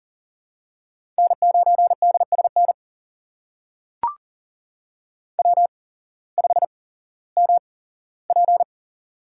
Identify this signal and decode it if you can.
Morse code, heard as a keyed tone, about 39 wpm, spaced as for 18 wpm, 700 Hz. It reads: N9DSN W5MP
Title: USA Call Signs - 8x Character Spacing - 40wpm